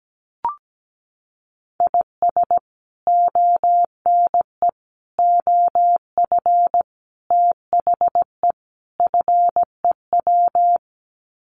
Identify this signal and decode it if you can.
Morse code, heard as a keyed tone, about 17 wpm, 700 Hz. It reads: IS ONE OF THE FEW
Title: Sets of 5 Words 17wpm